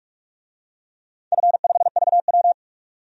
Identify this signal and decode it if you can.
Morse code, heard as a keyed tone, about 41 wpm, 700 Hz. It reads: F5VW
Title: DXCC-100 Call Signs - 4 Characters - 45wpm